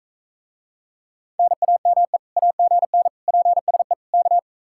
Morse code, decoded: NAME AGN PSE K

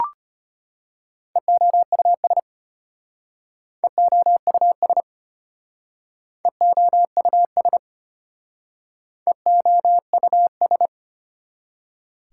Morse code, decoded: EOUS EOUS EOUS EOUS

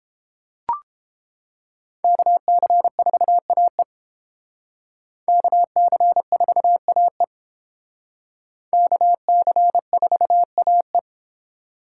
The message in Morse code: KC4AE KC4AE KC4AE

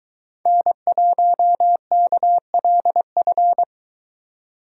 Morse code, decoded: N1KLF

